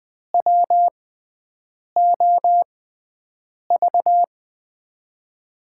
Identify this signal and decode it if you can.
Morse code, heard as a keyed tone, about 20 wpm, spaced as for 7 wpm, 700 Hz. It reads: WOV
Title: USA Call Signs - 5 Characters - 6x Character Spacing - 20wpm